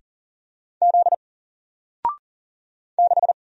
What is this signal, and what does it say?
Morse code, heard as a keyed tone, about 40 wpm, 700 Hz. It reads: Z 6
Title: Single Letter-Number Alt 40wpm